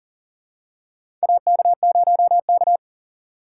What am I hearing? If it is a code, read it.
Morse code, heard as a keyed tone, about 40 wpm, 700 Hz. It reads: AK0K
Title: USA Call Signs - 4 Characters - 3x Character Spacing - 40wpm